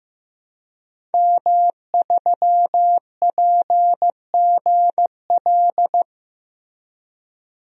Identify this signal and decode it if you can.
Morse code, heard as a keyed tone, about 15 wpm, 700 Hz. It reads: M3PGL
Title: DXCC-100 Call Signs - 5 Characters - Repeated 2x - 15wpm